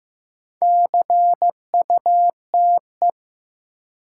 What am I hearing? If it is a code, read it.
Morse code, heard as a keyed tone, about 15 wpm, 700 Hz. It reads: CUTE
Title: Sentences from Top 1000 Words - Review Part 7 - New Word Drill - 15wpm